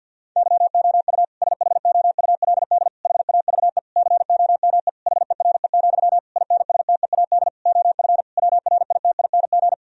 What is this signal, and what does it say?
Morse code, heard as a keyed tone, about 50 wpm, 700 Hz. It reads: YOU SHOULD HAVE COME HERE, INSTEAD OF WRITING
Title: Sentences from Top 1000 Words - Review Part 4 - 50wpm